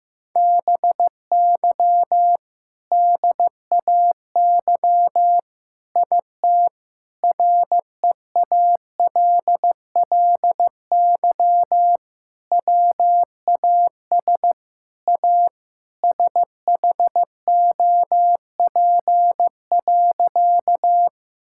Morse code, heard as a keyed tone, about 15 wpm, 700 Hz. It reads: BY DAY IT REALLY WAS A SHOP.
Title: Sentences from Top 1600 Words 15wpm